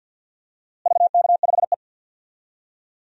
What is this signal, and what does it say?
Morse code, heard as a keyed tone, about 47 wpm, 700 Hz. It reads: VK5E
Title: DXCC-100 Call Signs - 4 Characters - Repeated 2x - 50wpm